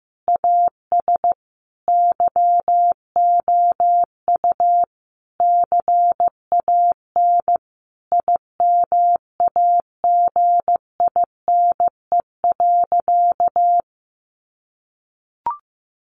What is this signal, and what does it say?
Morse code, heard as a keyed tone, about 15 wpm, 700 Hz. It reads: AS YOU CAN IMAGINE.
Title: Sentences from Top 1400 Words 15wpm